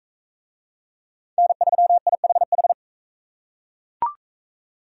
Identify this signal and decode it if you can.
Morse code, heard as a keyed tone, about 41 wpm, 700 Hz. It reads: N3IHH